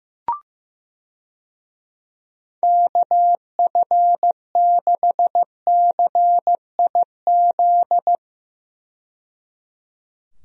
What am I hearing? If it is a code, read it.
Morse code, heard as a keyed tone, about 16 wpm, 700 Hz. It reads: KF6CIZ
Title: USA Call Signs - 6 Characters - 15wpm